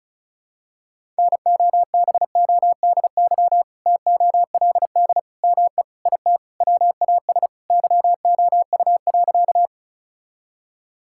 Morse code, heard as a keyed tone, about 34 wpm, 700 Hz. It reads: NOBODY TOLD ME IT WAS YOU.
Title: Sentences from Top 1600 Words 35wpm